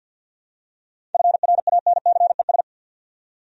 Morse code, heard as a keyed tone, about 48 wpm, 700 Hz. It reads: FRANCES
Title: USA Common Names - 50wpm